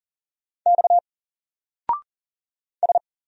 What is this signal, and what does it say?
Morse code, heard as a keyed tone, about 40 wpm, 700 Hz. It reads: X S
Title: Single Character (Letter:Number:Punctuation:Prosign) 40wpm